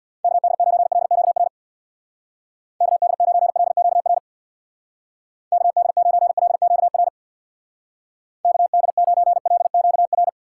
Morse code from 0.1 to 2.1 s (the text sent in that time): KD9LQR